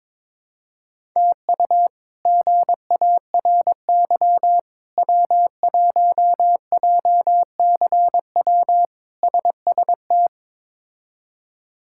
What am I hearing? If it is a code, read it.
Morse code, heard as a keyed tone, about 22 wpm, 700 Hz. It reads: TU GARY W1JCW SST
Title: K1USN SST QSO Practice - 22wpm